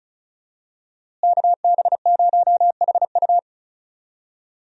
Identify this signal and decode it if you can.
Morse code, heard as a keyed tone, about 36 wpm, 700 Hz. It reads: KB0HU
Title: USA Call Signs - 5 Characters - Repeated 2x - 35wpm